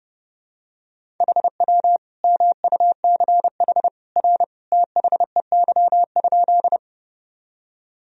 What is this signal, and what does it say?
Morse code, heard as a keyed tone, about 32 wpm, 700 Hz. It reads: HW MUCH R THEY?
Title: Sentences from Top 200 Words and Abbreviations 30wpm